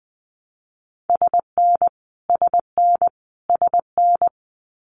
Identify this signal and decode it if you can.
Morse code, heard as a keyed tone, about 20 wpm, 700 Hz. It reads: SN SN SN
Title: Abbreviations Repeated 3x - 20wpm